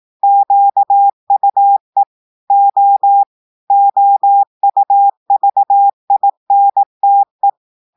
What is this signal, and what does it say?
Morse code, heard as a keyed tone, about 18 wpm, 800 Hz. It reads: QUE O OUVINTE
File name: audios / portugais / portugais texte en clair 2 vitesse 18 mots minutes